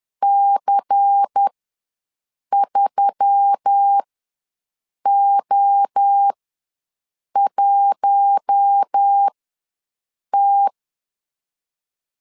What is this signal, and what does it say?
Morse code, heard as a keyed tone, about 10 wpm, spaced as for 6 wpm, 800 Hz. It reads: C3O1T
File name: LEZIONI / 5 LBVC / 5 LBVC ws 10wpm 3733